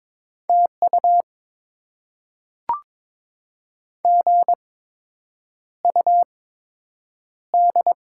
Morse code, heard as a keyed tone, about 22 wpm, spaced as for 8 wpm, 700 Hz. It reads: TU GUD
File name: Top 20 QSO Elements - 8x Character Spacing - 22wpm